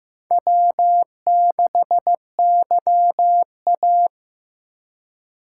Morse code, heard as a keyed tone, about 18 wpm, 700 Hz. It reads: W6YA
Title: Speed Racing - USA 1x2 Call Signs 15wpm